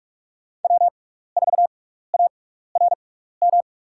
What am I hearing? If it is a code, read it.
Morse code, heard as a keyed tone, about 47 wpm, spaced as for 16 wpm, 700 Hz. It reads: W4ARM
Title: USA Call Signs - 5 Characters - 6x Character Spacing - 45wpm